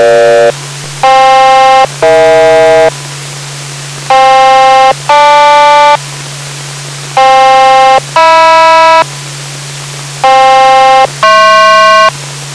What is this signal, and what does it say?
Aircraft SELCAL, heard as two pairs of tones. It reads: FK-EH FK-GL FK-HM FK-HP